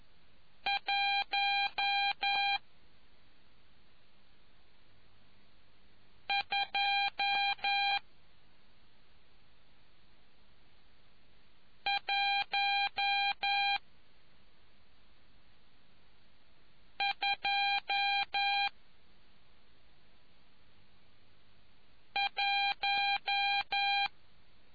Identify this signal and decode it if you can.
Morse code, heard as a keyed tone, about 11 wpm, spaced as for 2 wpm, 800 Hz. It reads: 12121